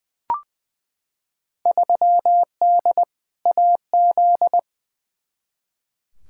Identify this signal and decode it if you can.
Morse code, heard as a keyed tone, about 20 wpm, 700 Hz. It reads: 3D AZ
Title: Field Day Exchanges - 20wpm